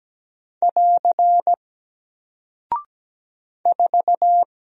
Morse code, heard as a keyed tone, about 17 wpm, 700 Hz. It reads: <AR> 4
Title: Single Character (Letter:Number:Punctuation:Prosign) Alt 17wpm